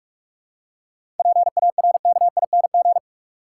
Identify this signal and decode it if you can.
Morse code, heard as a keyed tone, about 46 wpm, 700 Hz. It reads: PARKING
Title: Seven Letter Words - 45wpm